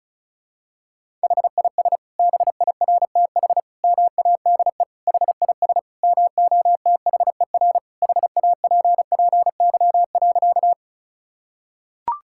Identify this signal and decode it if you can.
Morse code, heard as a keyed tone, about 34 wpm, 700 Hz. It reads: HIS BIRTH MADE HIS MOTHER HAPPY.
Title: Sentences from Top 1000 Words - Review Part 5 - 35wpm